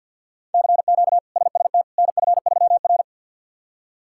Morse code, heard as a keyed tone, about 51 wpm, 700 Hz. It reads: CQ SST NF3R
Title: K1USN SST QSO Practice - 50wpm